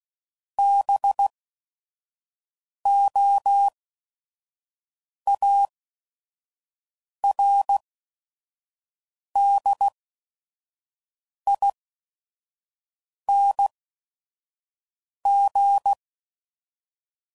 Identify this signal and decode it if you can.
Morse code, heard as a keyed tone, about 16 wpm, spaced as for 5 wpm, 800 Hz. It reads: BOARDING
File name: Suffixes / Special ING's / VerySlowToFaster ING's